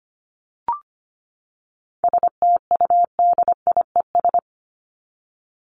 Morse code, heard as a keyed tone, about 25 wpm, 700 Hz. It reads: STUDIES
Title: ES:S Suffix - Top 100 Words with Base Words 25wpm